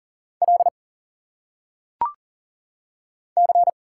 Morse code, heard as a keyed tone, about 40 wpm, 700 Hz. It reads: L C
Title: Single Letters - Phonetic Pronunciation 40wpm